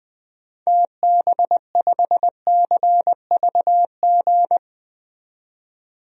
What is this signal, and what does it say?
Morse code, heard as a keyed tone, about 20 wpm, 700 Hz. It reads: TB5CVG
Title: DXCC-100 Call Signs - 6 Characters - 20wpm